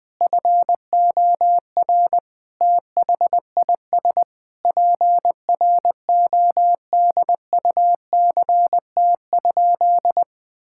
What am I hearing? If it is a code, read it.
Morse code, heard as a keyed tone, about 21 wpm, 700 Hz. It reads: FOR THIS PRODUCT?
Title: Sentences with 9-Word Phrases - 20wpm